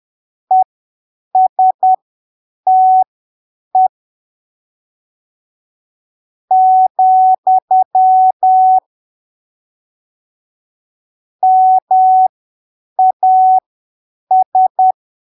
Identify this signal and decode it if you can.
Morse code, heard as a keyed tone, about 10 wpm, spaced as for 7 wpm, 750 Hz. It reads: ESTE , MAS